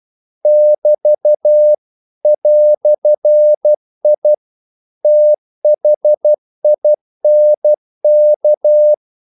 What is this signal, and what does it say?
Morse code, heard as a keyed tone, about 12 wpm, 600 Hz. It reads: <BT> "I THINK
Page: English quotations in CW (telegraphy - morse)